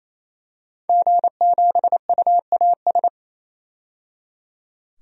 Morse code, heard as a keyed tone, about 28 wpm, 700 Hz. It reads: G7UAS
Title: DXCC-100 Call Signs - 5 Characters - 28wpm